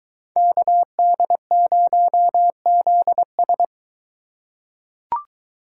Morse code, heard as a keyed tone, about 21 wpm, 700 Hz. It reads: KD0ZS